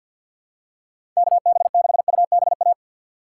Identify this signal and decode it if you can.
Morse code, heard as a keyed tone, about 52 wpm, 700 Hz. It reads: KB6UBA
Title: USA Call Signs - 2x Character Spacing - 50wpm